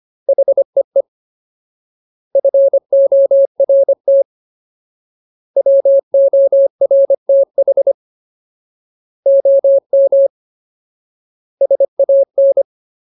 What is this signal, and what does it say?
Morse code, heard as a keyed tone, about 25 wpm, 550 Hz. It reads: HEE FORT WORTH OM SAN
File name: Ultimate Word List / 25WPM Ultimate